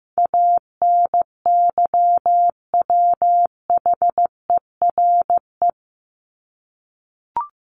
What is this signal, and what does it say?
Morse code, heard as a keyed tone, about 15 wpm, 700 Hz. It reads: ANYWHERE
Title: Top 1251-1500 Words 15wpm